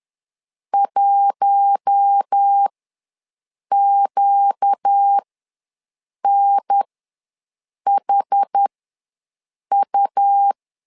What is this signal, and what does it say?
Morse code, heard as a keyed tone, about 10 wpm, spaced as for 6 wpm, 800 Hz. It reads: 1QNHU
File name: LEZIONI / 4 Q1 ws 10wpm 2538